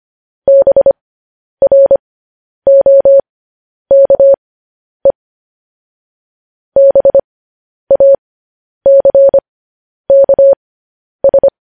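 Morse code, heard as a keyed tone, about 25 wpm, spaced as for 10 wpm, 550 Hz. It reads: BROKE BACKS
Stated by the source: Morse code news headlines sent as a reading transcription, preceded by vvv.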